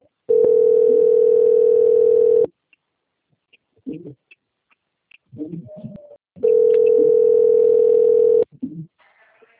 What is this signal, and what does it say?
Ringback tone (North America), 440 + 480 Hz.
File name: out- - - - - .17659